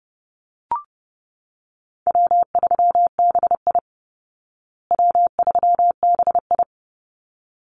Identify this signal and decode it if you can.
Morse code, heard as a keyed tone, about 30 wpm, 700 Hz. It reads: W3BI W3BI
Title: Easy Call Signs Repeated 2x - 30wpm